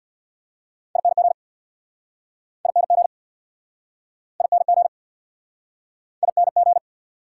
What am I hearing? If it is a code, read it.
Morse code, heard as a keyed tone, about 69 wpm, 690 Hz. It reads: ING ING ING ING